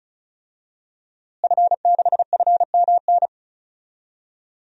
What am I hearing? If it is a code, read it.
Morse code, heard as a keyed tone, about 36 wpm, 700 Hz. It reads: F6FMN
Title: DXCC-100 Call Signs - 5 Characters - Repeated 3x - 35wpm